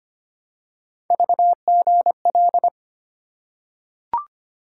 Morse code, heard as a keyed tone, about 25 wpm, 700 Hz. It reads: VGL
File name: Three Letter-Number Combinations - 25wpm